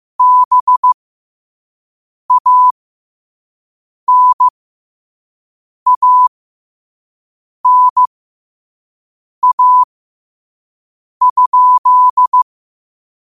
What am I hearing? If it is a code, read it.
Morse code, heard as a keyed tone, about 15 wpm, spaced as for 5 wpm, 1000 Hz. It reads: BANANA?
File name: qotdfiles / 5wpm